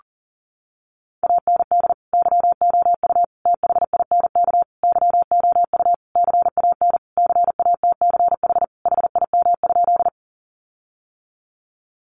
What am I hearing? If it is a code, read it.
Morse code, heard as a keyed tone, about 40 wpm, 700 Hz. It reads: AND YOU THINK YOU CAN CATCH HIM?
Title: Sentences from Top 1500 Words 40wpm